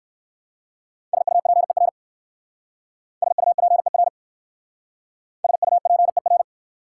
Morse code, heard as a keyed tone, about 57 wpm, 700 Hz. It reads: SUPER SUPER SUPER